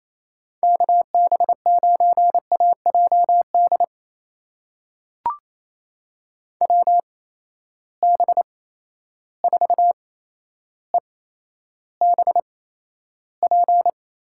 Morse code, heard as a keyed tone, about 28 wpm, spaced as for 12 wpm, 700 Hz. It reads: KB9AJD WB4EBP
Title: USA Call Signs - 8x Character Spacing - 28wpm